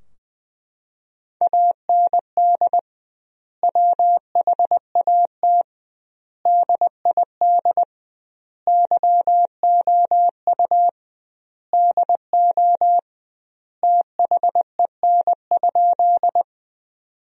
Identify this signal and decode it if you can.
Morse code, heard as a keyed tone, about 20 wpm, 700 Hz. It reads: AND WHAT DID YOU DO THEN?
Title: 2x Word-Spacing - Sentences from Top 100 Words - 20wpm